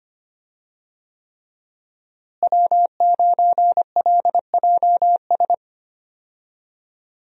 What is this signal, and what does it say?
Morse code, heard as a keyed tone, about 25 wpm, 700 Hz. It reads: W9LJS
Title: USA Call Signs - 5 Characters - Repeated 3x - 25wpm